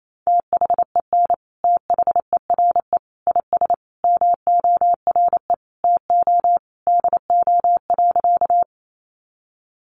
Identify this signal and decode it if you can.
Morse code, heard as a keyed tone, about 28 wpm, 700 Hz. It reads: THEN THERE IS MORE TO DO.
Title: Sentences from Top 100 Words 28wpm